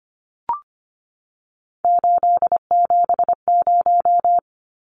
Morse code, heard as a keyed tone, about 25 wpm, 700 Hz. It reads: 870
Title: Numbers - 3-Digit 25wpm